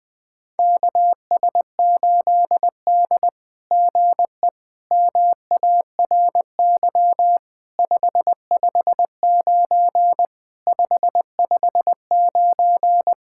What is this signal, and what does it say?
Morse code, heard as a keyed tone, about 20 wpm, 700 Hz. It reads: KS8D GE MARY 559 559
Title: SOTA QSO Practice 20wpm